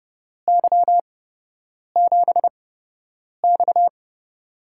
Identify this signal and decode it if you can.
Morse code, heard as a keyed tone, about 30 wpm, spaced as for 8 wpm, 700 Hz. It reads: Y7X
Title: USA Call Signs - 4 Characters - 8x Character Spacing - 30wpm